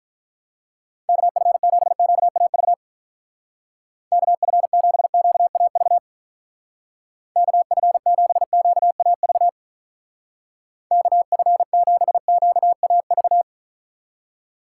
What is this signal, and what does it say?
Morse code, heard as a keyed tone, about 46 wpm, 700 Hz. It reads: KF7QAV KF7QAV KF7QAV KF7QAV